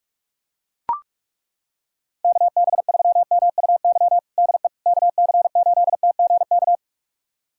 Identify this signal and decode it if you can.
Morse code, heard as a keyed tone, about 41 wpm, 700 Hz. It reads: KB3MUY DE KC8TGK
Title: USA Callsign DE Callsign - 6 Characters - 45wpm